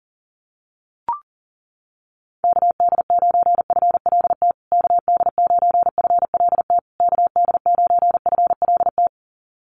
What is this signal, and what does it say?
Morse code, heard as a keyed tone, about 40 wpm, 700 Hz. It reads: KD9FLT KD9FLT KD9FLT
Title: USA Call Signs Repeated 3x 40wpm